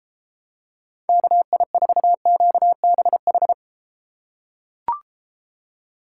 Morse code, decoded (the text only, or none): KI4QBH